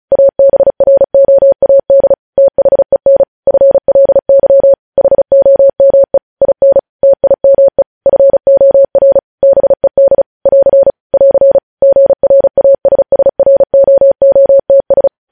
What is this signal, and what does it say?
Morse code, heard as a keyed tone, about 36 wpm, 550 Hz. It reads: ABROAD THEN FLY HOME IN TIME FOR BED <AR> <AR> GRASSROOTS
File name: podcast / News headlines-1x-35x35